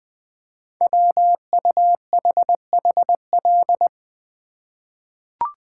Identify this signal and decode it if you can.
Morse code, heard as a keyed tone, about 20 wpm, 700 Hz. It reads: WUHHL
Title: Code Groups - 5 Letters - 20wpm